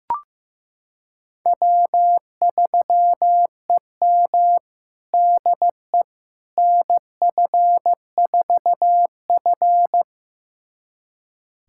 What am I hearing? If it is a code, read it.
Morse code, heard as a keyed tone, about 15 wpm, 700 Hz. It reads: W3EM DE NF4F
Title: USA Callsign DE Callsign - 4 Characters - 15wpm